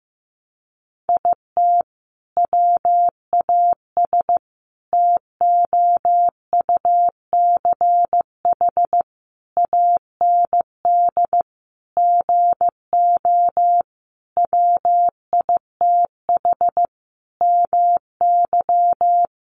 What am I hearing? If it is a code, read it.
Morse code, heard as a keyed tone, about 15 wpm, 700 Hz. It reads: IT WAS TOUCH AND GO WITH MY
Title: Sentences from Top 1200 Words 15wpm